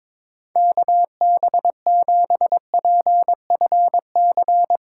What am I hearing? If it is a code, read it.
Morse code, heard as a keyed tone, about 22 wpm, 700 Hz. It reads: KB7PFC
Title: USA Call Signs - 6 Characters - 8x Character Spacing - 22wpm